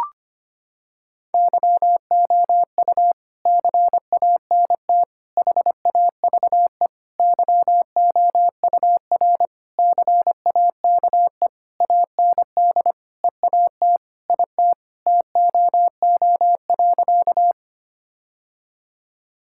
Morse code, heard as a keyed tone, about 25 wpm, 700 Hz. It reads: YOU CANT HAVE YOUR CAKE AND EAT IT TOO.
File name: practice / 25wpm / Sentences with 9-Word Phrases - 25wpm